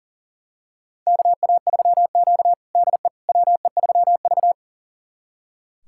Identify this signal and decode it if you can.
Morse code, heard as a keyed tone, about 40 wpm, 700 Hz. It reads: KA3Q DE WE3V
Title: USA Callsign DE Callsign - 4 Characters - 40wpm